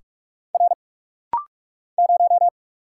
Morse code, heard as a keyed tone, about 46 wpm, 700 Hz. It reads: R 0
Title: Single Letter-Number Alt - Rapid-Fire 45wpm